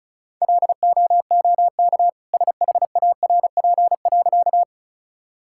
Morse code, heard as a keyed tone, about 34 wpm, 700 Hz. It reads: LOOK SHARP.
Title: Sentences from Top 1600 Words 35wpm